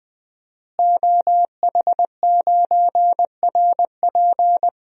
Morse code, heard as a keyed tone, about 20 wpm, 700 Hz. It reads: OH9RP
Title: DXCC-100 Call Signs - 5 Characters - Repeated 2x - 20wpm